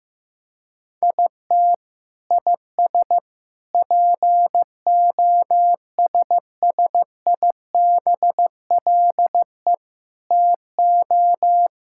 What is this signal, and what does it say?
Morse code, heard as a keyed tone, about 16 wpm, 700 Hz. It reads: IT IS POSSIBLE TO
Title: Sentences from Top 1000 Words - Review Part 6 - 15wpm